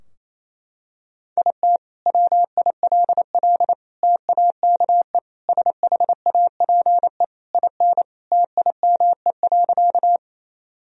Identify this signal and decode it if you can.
Morse code, heard as a keyed tone, about 28 wpm, 700 Hz. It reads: IT WILL TAKE SHAPE IN TIME.
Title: Sentences from Top 1000 Words - Review Part 3 - 28wpm